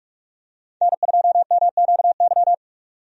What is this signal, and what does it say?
Morse code, heard as a keyed tone, about 44 wpm, 700 Hz. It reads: N2MQY
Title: USA Call Signs - 5 Characters - 45wpm